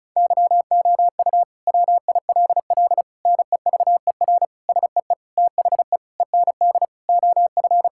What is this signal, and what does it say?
Morse code, heard as a keyed tone, about 36 wpm, 700 Hz. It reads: YOU WILL NEVER SEE THE END OF
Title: Sentences from Top 1600 Words 35wpm